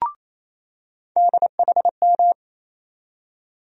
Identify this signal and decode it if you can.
Morse code, heard as a keyed tone, about 28 wpm, 700 Hz. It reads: DHM
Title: USA Call Sign Postfixes - 3 Letters 28wpm